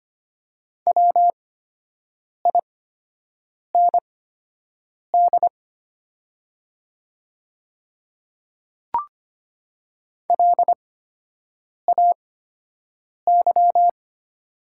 Morse code, heard as a keyed tone, about 25 wpm, spaced as for 7 wpm, 700 Hz. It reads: WIND LAY
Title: 8x Character Spacing - Top 500 Words - 25wpm